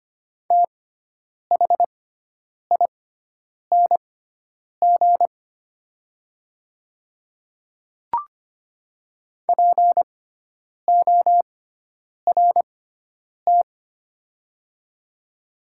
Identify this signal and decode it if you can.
Morse code, heard as a keyed tone, about 25 wpm, spaced as for 9 wpm, 700 Hz. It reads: THING PORT
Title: 6x Character Spacing - Top 200 Words - 25wpm